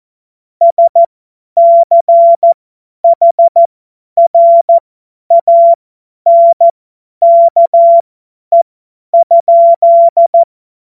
Morse code, heard as a keyed tone, about 14 wpm, spaced as for 10 wpm, 680 Hz. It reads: SCHRANKE?
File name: Morsekurs / Witze / Witz 28. Dezember 2021, 50 F70